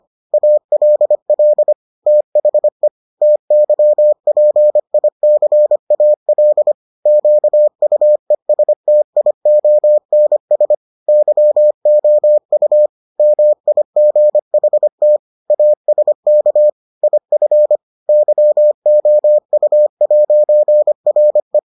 Morse code, heard as a keyed tone, about 25 wpm, 600 Hz. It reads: ALL THE TYPICAL QUESTIONS YOU MIGHT ASK IF YOU'RE